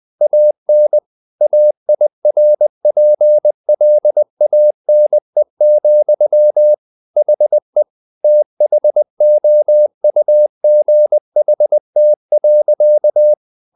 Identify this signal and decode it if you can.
Morse code, heard as a keyed tone, about 20 wpm, 600 Hz. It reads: AN AIRPLANE, HE THOUGHT.
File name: the old man and the sea part 2 20wpm